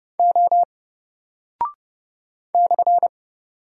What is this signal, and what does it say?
Morse code, heard as a keyed tone, about 29 wpm, 700 Hz. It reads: O /
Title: Single Character (Letter:Number:Punctuation:Prosign) 30wpm